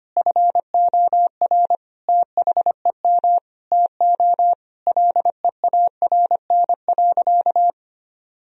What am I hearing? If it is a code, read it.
Morse code, heard as a keyed tone, about 25 wpm, 700 Hz. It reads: FOR THEM TO LEARN.
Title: Sentences from Top 1000 Words - Review Part 3 - 25wpm